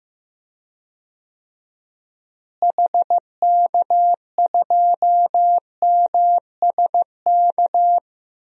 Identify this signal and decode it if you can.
Morse code, heard as a keyed tone, about 15 wpm, 700 Hz. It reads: HK2MSK
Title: DXCC-100 Call Signs - 6 Characters - Repeated 3x - 15wpm